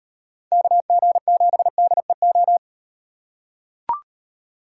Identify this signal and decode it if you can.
Morse code, heard as a keyed tone, about 37 wpm, 700 Hz. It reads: KG7DEO